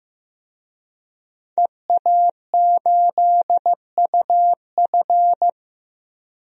Morse code, heard as a keyed tone, about 15 wpm, 700 Hz. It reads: EA8UF
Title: DXCC-100 Call Signs - 5 Characters - 15wpm